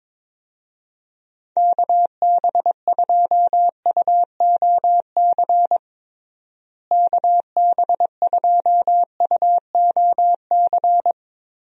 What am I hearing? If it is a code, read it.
Morse code, heard as a keyed tone, about 21 wpm, 700 Hz. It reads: KB2UOC KB2UOC
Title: USA Call Signs - 6 Characters - Repeated 2x - 22wpm